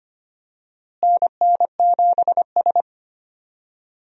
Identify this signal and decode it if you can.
Morse code, heard as a keyed tone, about 25 wpm, 700 Hz. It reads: NN7S
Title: USA Call Signs - 4 Characters - Repeated 2x - 25wpm